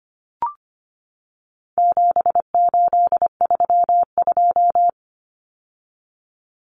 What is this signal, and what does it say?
Morse code, heard as a keyed tone, about 25 wpm, 700 Hz. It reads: 7832
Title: Common Cut Numbers - 4-Digit 25wpm